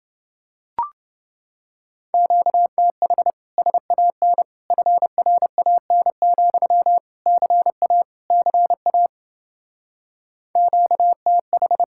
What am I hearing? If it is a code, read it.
Morse code, heard as a keyed tone, about 29 wpm, 700 Hz. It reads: QTH SAN FRAN, CA CA QTH